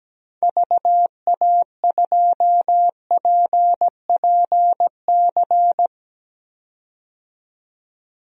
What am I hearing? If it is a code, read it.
Morse code, heard as a keyed tone, about 17 wpm, 700 Hz. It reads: VA2PPC
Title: DXCC-100 Call Signs - 6 Characters - 17wpm